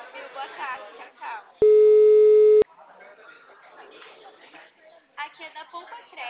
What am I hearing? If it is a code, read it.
Ringback tone (Europe, 425 Hz), with speech.